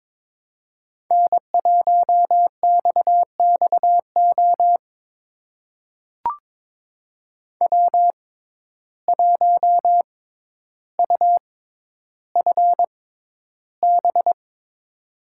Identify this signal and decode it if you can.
Morse code, heard as a keyed tone, about 22 wpm, spaced as for 11 wpm, 700 Hz. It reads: N1XXO W1UFB
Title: USA Call Signs - 5 Characters - 6x Character Spacing - 22wpm